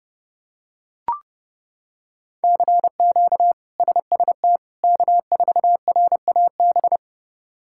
Morse code, heard as a keyed tone, about 30 wpm, 700 Hz. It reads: CQ SST K4RAB
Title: K1USN SST QSO Practice - 30wpm